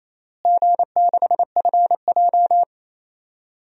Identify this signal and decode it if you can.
Morse code, heard as a keyed tone, about 28 wpm, 700 Hz. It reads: G6FJ
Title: DXCC-100 Call Signs - 4 Characters - Repeated 3x - 28wpm